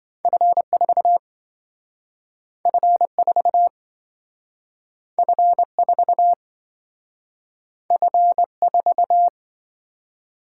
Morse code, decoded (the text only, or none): F4 F4 F4 F4